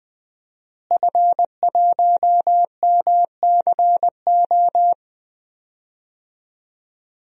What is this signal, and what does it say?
Morse code, heard as a keyed tone, about 20 wpm, 700 Hz. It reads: F1MCO
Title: DXCC-100 Call Signs - 5 Characters - Repeated 2x - 20wpm